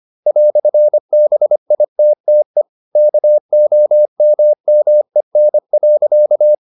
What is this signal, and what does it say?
Morse code, heard as a keyed tone, about 25 wpm, 600 Hz. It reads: "BITTE KOMMEN.